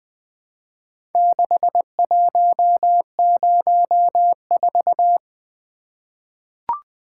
Morse code, heard as a keyed tone, about 20 wpm, 700 Hz. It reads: 6104
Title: Four Letter-Number Combinations - 20wpm